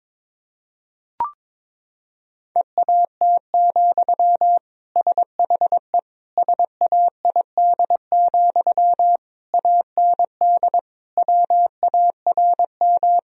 Morse code, heard as a keyed tone, about 22 wpm, 700 Hz. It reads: EAT, SHE SAID, AND WARM
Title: Sentences from Top 1000 Words - Review Part 2 - 22wpm